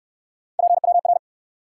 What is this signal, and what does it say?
Morse code, heard as a keyed tone, about 68 wpm, 700 Hz. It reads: PWR